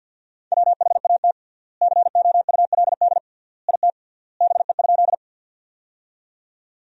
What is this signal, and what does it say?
Morse code, heard as a keyed tone, about 47 wpm, 700 Hz. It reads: WHAT COULD IT BE?
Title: 3x Word-Spacing - Sentences from Top 100 Words - 50wpm